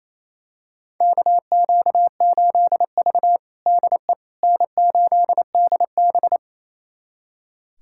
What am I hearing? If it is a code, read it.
Morse code, heard as a keyed tone, about 28 wpm, 700 Hz. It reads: KQ8V DE N8DB
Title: USA Callsign DE Callsign - 4 Characters - 28wpm